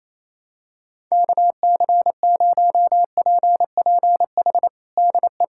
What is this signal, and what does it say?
Morse code, heard as a keyed tone, about 28 wpm, 700 Hz. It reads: KC0PPH DE
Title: USA Callsign DE Callsign 28wpm